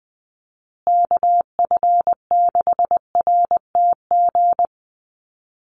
Morse code, heard as a keyed tone, about 20 wpm, 700 Hz. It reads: KF6RTG
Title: USA Call Signs Repeated 3x 20wpm